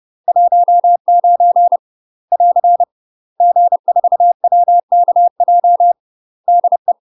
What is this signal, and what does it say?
Morse code, heard as a keyed tone, about 30 wpm, 700 Hz. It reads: 19 <AR> G4WKJ DE
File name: UK MASTER QSOs TRAINING / 30WPM / UK MASTER QSOs 0025